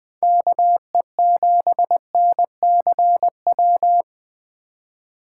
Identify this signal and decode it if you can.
Morse code, heard as a keyed tone, about 20 wpm, 700 Hz. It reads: KE7NCW
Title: USA Call Signs - 6 Characters - 4x Character Spacing - 20wpm